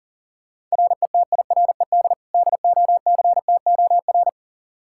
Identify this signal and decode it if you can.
Morse code, heard as a keyed tone, about 40 wpm, 700 Hz. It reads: RETIRED DOCTOR
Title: Practice QSO - Exchange 3 - More Info - 40wpm